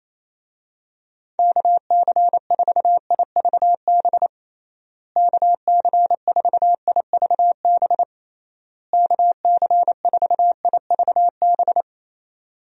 Morse code, heard as a keyed tone, about 28 wpm, 700 Hz. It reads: KC4IVB KC4IVB KC4IVB
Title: USA Call Signs - 6 Characters - Repeated 3x - 28wpm